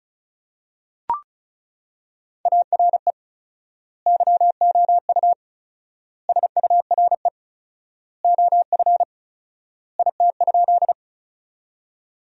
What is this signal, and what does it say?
Morse code, heard as a keyed tone, about 36 wpm, 700 Hz. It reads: ARE YOU SURE OF IT?
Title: 4x Word-Spacing - Sentences from Top 300 Words - 35wpm